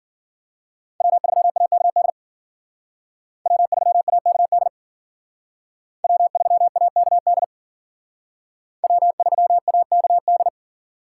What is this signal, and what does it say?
Morse code, heard as a keyed tone, about 51 wpm, 700 Hz. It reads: W3AKD W3AKD W3AKD W3AKD